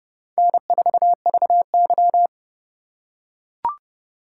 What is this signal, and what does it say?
Morse code, heard as a keyed tone, about 30 wpm, 700 Hz. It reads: N4VY
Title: USA Call Signs - 4 Characters - 2x Character Spacing - 30wpm